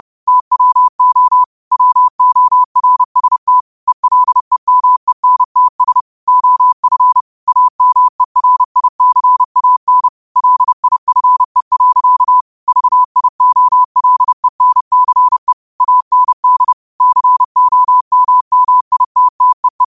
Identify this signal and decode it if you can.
Morse code, heard as a keyed tone, about 30 wpm, 1000 Hz. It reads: TWO WORST ELEMENTS OF AMERICAN LIFE. VIOLENCE AND COMMITTEE
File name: qotdfiles / 30wpm